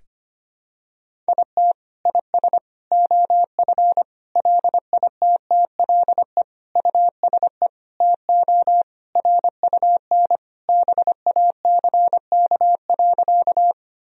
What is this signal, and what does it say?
Morse code, heard as a keyed tone, about 25 wpm, 700 Hz. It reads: IT IS OF LITTLE USE TO RUN BACK.
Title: Sentences from Top 1000 Words - Review Part 1 - 25wpm